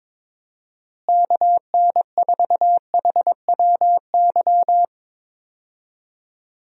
Morse code, heard as a keyed tone, about 22 wpm, 700 Hz. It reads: KN4HWY
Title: USA Call Signs - 6 Characters - Repeated 3x - 22wpm